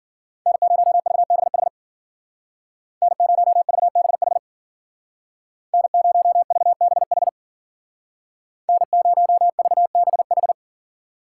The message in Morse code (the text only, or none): N0VBH N0VBH N0VBH N0VBH